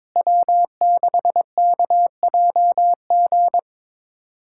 Morse code, heard as a keyed tone, about 22 wpm, 700 Hz. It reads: W6KJG
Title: USA Callsign DE Callsign - 5 Characters - 22wpm